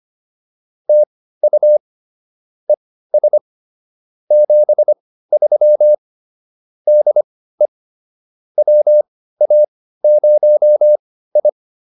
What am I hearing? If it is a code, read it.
Morse code, heard as a keyed tone, about 25 wpm, spaced as for 15 wpm, 590 Hz. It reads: TU ES 73 DE WA0I